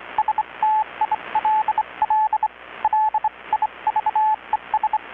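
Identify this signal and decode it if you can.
Morse code, heard as a keyed tone, about 25 wpm, spaced as for 22 wpm, 840 Hz, with radio noise. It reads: STILL LIVES
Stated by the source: Maritime radiotelegraphy coast station event transmission, received in Chiapas, Mexico about 3,800 km away on a Kenwood R-600 receiver with a 10-metre grounded longwire antenna.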